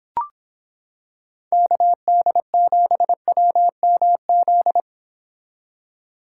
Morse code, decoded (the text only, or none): KD7WMZ